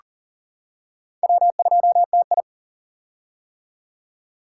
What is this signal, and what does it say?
Morse code, heard as a keyed tone, about 40 wpm, 700 Hz. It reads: W2TI
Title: K1USN SST QSO Practice - 40wpm